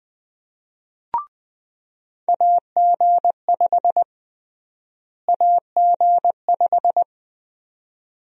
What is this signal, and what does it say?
Morse code, heard as a keyed tone, about 20 wpm, 700 Hz. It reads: AG5 AG5
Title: USA Call Sign Prefixes - Repeated 2x - 20wpm